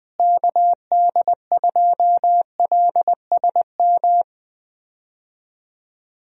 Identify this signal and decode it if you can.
Morse code, heard as a keyed tone, about 20 wpm, 700 Hz. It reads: KD2LSM
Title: USA Call Signs - 6 Characters - Repeated 3x - 20wpm